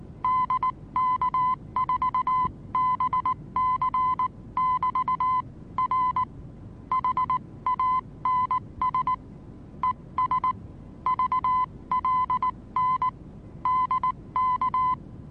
Morse code, heard as a keyed tone, about 19 wpm, spaced as for 16 wpm, 1020 Hz. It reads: DK4BC<BT>R HANS ES VLN DK
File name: repos / Morsekurs / DJ4UF Morsen26